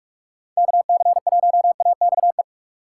Morse code, heard as a keyed tone, about 46 wpm, 700 Hz. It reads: KC1AXE
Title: USA Call Signs - 6 Characters - 6x Character Spacing - 45wpm